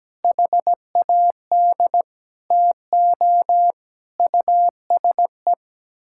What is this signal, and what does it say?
Morse code, heard as a keyed tone, about 17 wpm, 700 Hz. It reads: HAD TO USE
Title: Sentences from Top 1000 Words - Review Part 6 - 17wpm